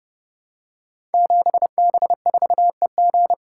Morse code, heard as a keyed tone, about 28 wpm, 700 Hz. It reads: 7B4EG
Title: Code Groups - 5 Letter:Numbers - 30wpm